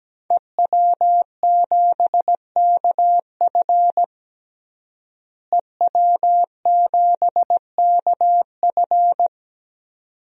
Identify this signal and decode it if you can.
Morse code, heard as a keyed tone, about 17 wpm, 700 Hz. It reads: EW7KF EW7KF
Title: DXCC-100 Call Signs - 5 Characters - Repeated 2x - 17wpm